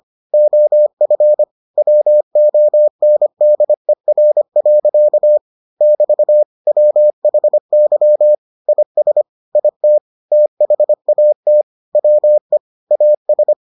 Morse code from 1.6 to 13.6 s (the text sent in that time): WONDER. <BT> WHY IS IT THAT WE AS